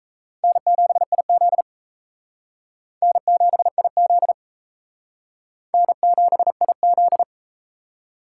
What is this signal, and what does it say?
Morse code, heard as a keyed tone, about 38 wpm, 700 Hz. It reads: N7IZ N7IZ N7IZ